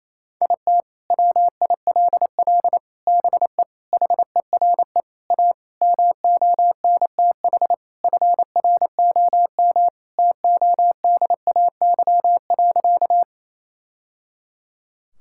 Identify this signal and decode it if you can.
Morse code, heard as a keyed tone, about 28 wpm, 700 Hz. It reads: IT WILL BE HERE A MONTH FROM TODAY.
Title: Sentences from Top 1000 Words - Review Part 5 - 28wpm